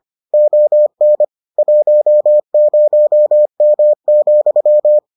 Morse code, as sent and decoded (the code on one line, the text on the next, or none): --- -. / .---- ----- -- --..--
ON 10M,